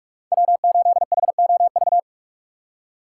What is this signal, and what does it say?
Morse code, heard as a keyed tone, about 46 wpm, 700 Hz. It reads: W8HOV
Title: USA Call Signs - 3x Character Spacing - 45wpm